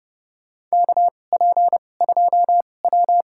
Morse code, heard as a keyed tone, about 30 wpm, spaced as for 22 wpm, 700 Hz. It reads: KP2W
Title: USA Call Signs - 2x Character Spacing - 30wpm